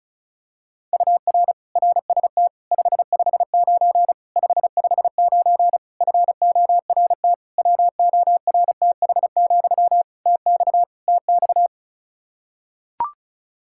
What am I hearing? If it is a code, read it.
Morse code, heard as a keyed tone, about 34 wpm, 700 Hz. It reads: UR RST 559 559 FORT WORTH, TX TX
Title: Practice QSO - Exchange 1 - Beginning - 35wpm